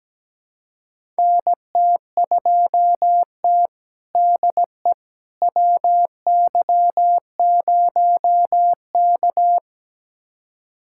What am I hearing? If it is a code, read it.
Morse code, heard as a keyed tone, about 17 wpm, 700 Hz. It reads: NT2T DE WY0K
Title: USA Callsign DE Callsign - 4 Characters - 17wpm